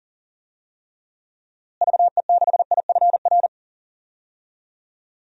Morse code, heard as a keyed tone, about 40 wpm, 700 Hz. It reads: VE6IFR
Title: DXCC-100 Call Signs - 6 Characters - Repeated 3x - 40wpm